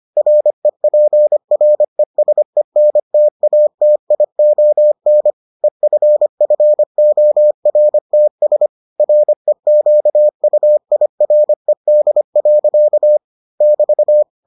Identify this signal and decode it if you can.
Morse code, heard as a keyed tone, about 25 wpm, 600 Hz. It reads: REPRESENTATION EFFORTS REQUIRED. <BT>